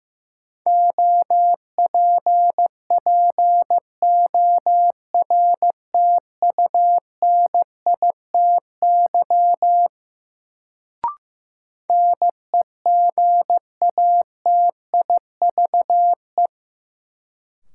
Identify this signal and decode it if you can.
Morse code, heard as a keyed tone, about 15 wpm, 700 Hz. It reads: OPPORTUNITY NEGATIVE
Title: Sentences from Top 1000 Words - New Word Drill - 15wpm